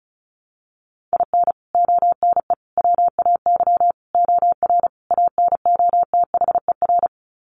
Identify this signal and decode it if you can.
Morse code, heard as a keyed tone, about 34 wpm, 700 Hz. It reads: IN ONE WAY OR ANOTHER
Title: Sets of 5 Words Encore 35wpm